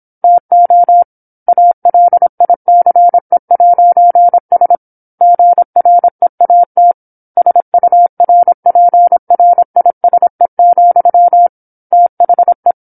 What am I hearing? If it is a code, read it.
Morse code, heard as a keyed tone, about 27 wpm, 700 Hz. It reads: TO ALICE'S GREAT SURPRISE, THE
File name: 08-Chapter-08 26 WPM